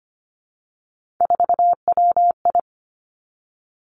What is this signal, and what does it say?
Morse code, heard as a keyed tone, about 25 wpm, 700 Hz. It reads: 4WI
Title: Three Challenging Letter-Number Combinations 25wpm